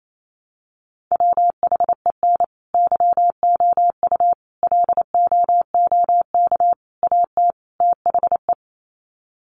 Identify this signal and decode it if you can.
Morse code, heard as a keyed tone, about 28 wpm, 700 Hz. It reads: WHEN YOU LOOK AT THE
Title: Sets of 5 Words 28wpm